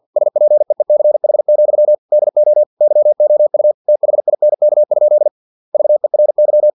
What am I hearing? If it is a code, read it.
Morse code, heard as a keyed tone, about 50 wpm, 600 Hz. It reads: SPEECH, DO YOU THINK? VERY
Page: Never 49wpm